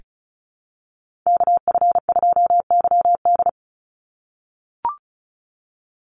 Morse code, heard as a keyed tone, about 34 wpm, 700 Hz. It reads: KF2YD
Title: USA Call Signs 35wpm